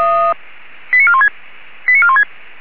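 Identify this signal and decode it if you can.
Five-tone selective call (ZVEI), heard as quick runs of five tones, tones of about 70 ms. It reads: 89416 89416
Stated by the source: ZVEI five-tone selective call 89415 received with an SDR.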